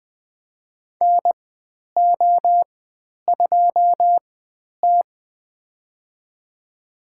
Morse code, heard as a keyed tone, about 20 wpm, spaced as for 10 wpm, 700 Hz. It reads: NO2T
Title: 095 Lesson 16g - Letter c - Call Signs 20wpm